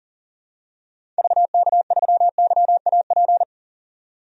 Morse code, heard as a keyed tone, about 40 wpm, 700 Hz. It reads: VK3YAP
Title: DXCC-100 Call Signs - 6 Characters - 40wpm